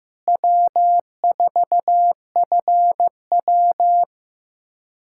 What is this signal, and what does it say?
Morse code, heard as a keyed tone, about 15 wpm, 700 Hz. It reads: W4FW
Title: USA Callsign DE Callsign - 4 Characters - 15wpm